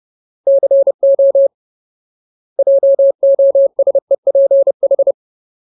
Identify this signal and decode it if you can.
Morse code, heard as a keyed tone, about 30 wpm, 550 Hz. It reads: CO JOSEPH
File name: Ultimate Word List / 30WPM Ultimate